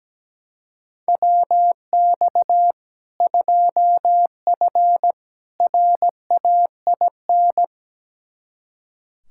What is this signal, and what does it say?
Morse code, heard as a keyed tone, about 17 wpm, 700 Hz. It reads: WX 2F RAIN
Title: K1USN SST QSO Practice 17wpm